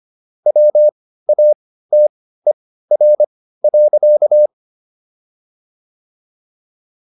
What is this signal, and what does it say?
Morse code, heard as a keyed tone, about 25 wpm, spaced as for 15 wpm, 600 Hz. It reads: WATER.